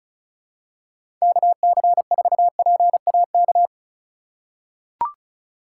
Morse code, heard as a keyed tone, about 36 wpm, 700 Hz. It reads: KC4PAK
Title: USA Call Signs - 6 Characters - 2x Character Spacing - 35wpm